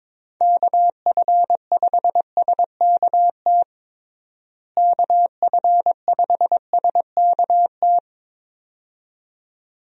Morse code, heard as a keyed tone, about 22 wpm, 700 Hz. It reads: KF5SKT KF5SKT
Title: USA Call Signs - 6 Characters - Repeated 2x - 22wpm